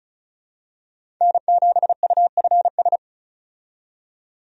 Morse code, heard as a keyed tone, about 36 wpm, 700 Hz. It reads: N7UFS
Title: USA Call Signs - 5 Characters - 35wpm